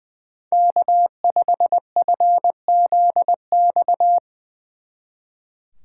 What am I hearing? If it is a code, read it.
Morse code, heard as a keyed tone, about 22 wpm, 700 Hz. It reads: K5FZX
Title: Speed Racing - USA 1x3 Call Signs 20wpm